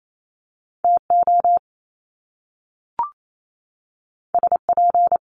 Morse code, heard as a keyed tone, about 28 wpm, 700 Hz. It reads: TO SP
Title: Top Two Letter Combinations in Words 28wpm